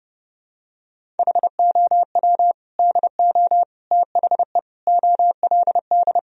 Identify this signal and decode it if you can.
Morse code, heard as a keyed tone, about 32 wpm, 700 Hz. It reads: HOW DO THE OLD
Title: Sentences from Top 1000 Words - Review Part 1 - 30wpm